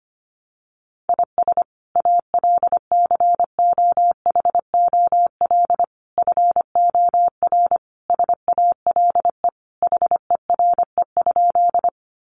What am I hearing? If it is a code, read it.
Morse code, heard as a keyed tone, about 25 wpm, 700 Hz. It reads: IS ALCOHOL FOR SALE HERE?
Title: Sentences from Top 1400 Words 25wpm